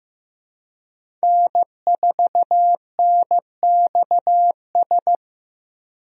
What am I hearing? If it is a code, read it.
Morse code, heard as a keyed tone, about 15 wpm, 700 Hz. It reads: N4NXS
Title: USA Call Signs - 5 Characters - Repeated 2x - 15wpm